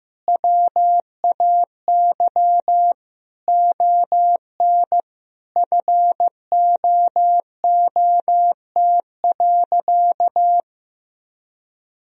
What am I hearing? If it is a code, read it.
Morse code, heard as a keyed tone, about 15 wpm, 700 Hz. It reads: WAY ON FOOT.
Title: Sentences from Top 1000 Words - Review Part 3 - 15wpm